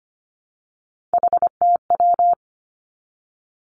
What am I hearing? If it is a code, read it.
Morse code, heard as a keyed tone, about 25 wpm, 700 Hz. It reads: HTW
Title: Three Challenging Letter-Number Combinations 25wpm